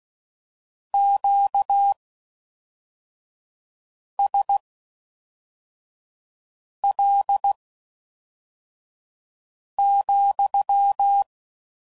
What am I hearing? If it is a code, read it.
Morse code, heard as a keyed tone, about 16 wpm, spaced as for 4 wpm, 800 Hz. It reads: QSL,